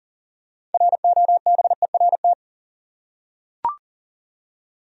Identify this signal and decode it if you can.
Morse code, heard as a keyed tone, about 39 wpm, 700 Hz. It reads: ROBERT
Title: USA Common Names - 40wpm